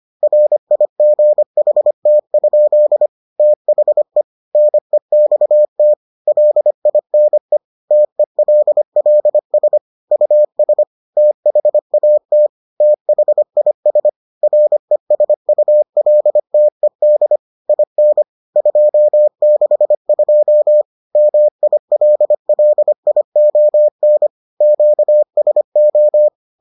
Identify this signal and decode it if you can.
Morse code, heard as a keyed tone, about 25 wpm, 600 Hz. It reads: RIGHT? THE NEXT LINE TELLS US THAT THIS RESULTED IN 262 MILLION QSO